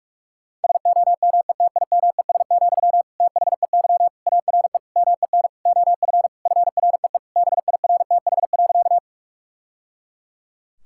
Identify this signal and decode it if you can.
Morse code, heard as a keyed tone, about 46 wpm, 700 Hz. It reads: SOMETIMES, THEY ARE MEN OF FREE BIRTH.
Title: Sentences from Top 1000 Words - Review Part 5 - 45wpm